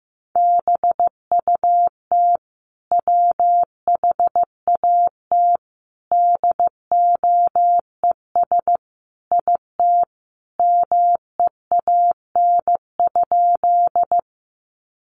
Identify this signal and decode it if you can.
Morse code, heard as a keyed tone, about 15 wpm, 700 Hz. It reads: BUT WHAT DOES IT MEAN?
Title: Sentences from Top 200 Words 15wpm